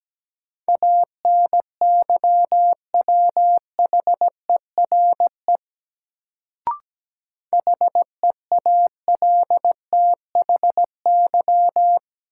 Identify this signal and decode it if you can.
Morse code, heard as a keyed tone, about 17 wpm, 700 Hz. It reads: ANYWHERE HEALTHY
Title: Sentences from Top 1400 Words - New Word Drill Alt - 17wpm